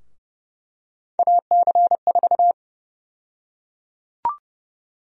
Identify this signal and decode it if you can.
Morse code, heard as a keyed tone, about 30 wpm, 700 Hz. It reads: AC4
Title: USA Call Sign Prefixes - 30wpm